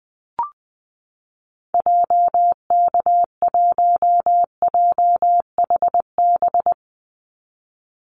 Morse code, heard as a keyed tone, about 20 wpm, 700 Hz. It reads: JK1JHB
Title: Call Signs Repeated 2x - 20wpm